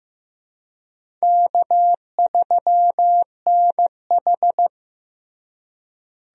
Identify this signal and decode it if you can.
Morse code, heard as a keyed tone, about 16 wpm, 700 Hz. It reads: K3NH
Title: USA Call Signs - 4 Characters - Repeated 3x - 15wpm